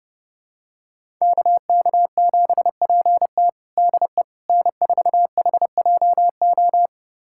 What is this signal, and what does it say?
Morse code, heard as a keyed tone, about 32 wpm, 700 Hz. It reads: KK7PT DE N4HJO
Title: USA Callsign DE Callsign - 5 Characters - 30wpm